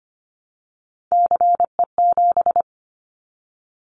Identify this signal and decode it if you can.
Morse code, heard as a keyed tone, about 25 wpm, 700 Hz. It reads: CE7
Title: Callsign Prefixes for Top 100 DXCC - 25wpm